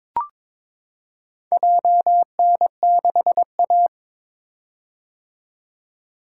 Morse code, heard as a keyed tone, about 21 wpm, 700 Hz. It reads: JN6A
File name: practice / 22wpm / Four Letter-Number Combinations - 22wpm